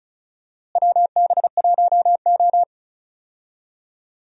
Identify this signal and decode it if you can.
Morse code, heard as a keyed tone, about 36 wpm, 700 Hz. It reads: WB1O
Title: USA Call Signs - 4 Characters - Repeated 3x - 35wpm